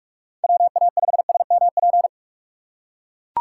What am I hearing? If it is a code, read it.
Morse code, heard as a keyed tone, about 43 wpm, 700 Hz. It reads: WA5SMP
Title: USA Call Signs - 6 Characters - 6x Character Spacing - 45wpm